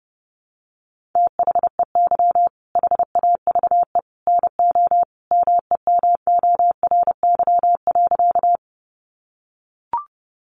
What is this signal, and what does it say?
Morse code, heard as a keyed tone, about 28 wpm, 700 Hz. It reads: THEY HAVE NO MEMORY.
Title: Sentences from Top 1300 Words 30wpm